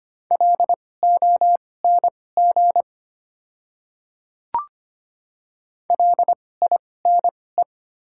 Morse code, heard as a keyed tone, about 25 wpm, spaced as for 18 wpm, 700 Hz. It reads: LONG LINE
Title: 2x Character Spacing - Top 200 Words - 25wpm